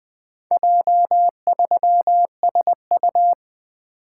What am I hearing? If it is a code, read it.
Morse code, heard as a keyed tone, about 20 wpm, 700 Hz. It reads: J3SU
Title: Four Letter-Number Combinations - 20wpm